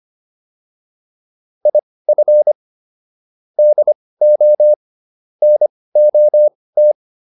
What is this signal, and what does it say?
Morse code, heard as a keyed tone, about 25 wpm, spaced as for 18 wpm, 600 Hz. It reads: IF DO NOT
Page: intermediate-practice-files